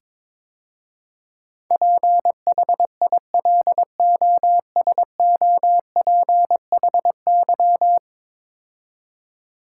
Morse code, heard as a keyed tone, about 22 wpm, 700 Hz. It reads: PHILOSOPHY
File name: practice / 22wpm / 10 Letter Words - First 3 Letters Given - 22wpm